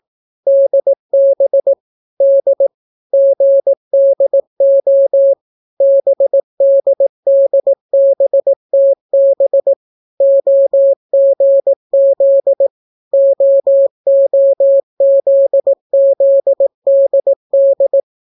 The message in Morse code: DB D GDO BDDBTB OGZ OOZZDD

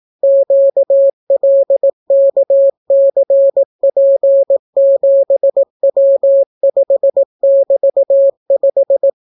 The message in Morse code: QLKCP7W5<BT>5

